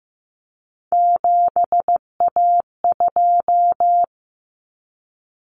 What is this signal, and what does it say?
Morse code, heard as a keyed tone, about 15 wpm, 700 Hz. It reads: 7A2
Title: Common Cut Numbers - 3-Digit 15wpm